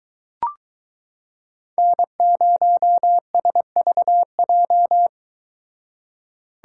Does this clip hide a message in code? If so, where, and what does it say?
Yes: Morse code from 1.8 to 5.1 s, heard as a keyed tone, about 22 wpm, 700 Hz. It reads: N0SVJ